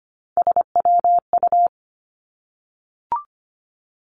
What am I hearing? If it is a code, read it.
Morse code, heard as a keyed tone, about 25 wpm, 700 Hz. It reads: SWU
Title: Three Challenging Letter-Number Combinations 25wpm